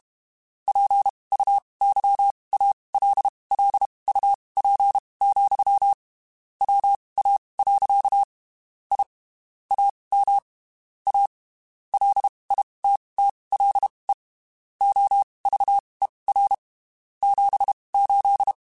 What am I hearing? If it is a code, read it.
Morse code, heard as a keyed tone, about 32 wpm, spaced as for 23 wpm, 800 Hz. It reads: PUYALLUP, WA. I AM A LITTLE OVER 78
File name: (spaced) / QSOs Spaced-mp3s / WA7QOX 2 @30 sp